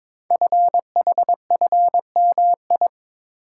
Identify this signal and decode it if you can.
Morse code, heard as a keyed tone, about 23 wpm, 700 Hz. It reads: FHFMI
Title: Code Groups - 5 Letter:Numbers - 22wpm